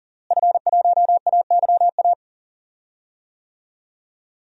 Morse code, heard as a keyed tone, about 40 wpm, 700 Hz. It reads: F1AYA
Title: DXCC-100 Call Signs - 5 Characters - Repeated 2x - 40wpm